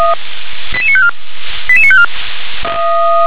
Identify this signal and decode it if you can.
Five-tone selective call (ZVEI), heard as quick runs of five tones, tones of about 70 ms. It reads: 88954 88954
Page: SDR/ZVEI